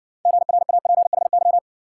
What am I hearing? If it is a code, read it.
Morse code, heard as a keyed tone, about 60 wpm, 700 Hz. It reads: GRAPHY